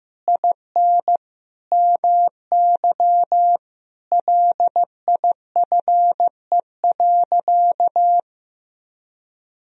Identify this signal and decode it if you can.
Morse code, heard as a keyed tone, about 15 wpm, 700 Hz. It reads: IN MY LIFE.
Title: Sentences from Top 1000 Words - Review Part 3 - 15wpm